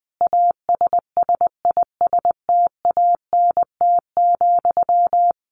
Morse code, heard as a keyed tone, about 20 wpm, 700 Hz. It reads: ASSISTANT,
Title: A Sound Of Thunder 20wpm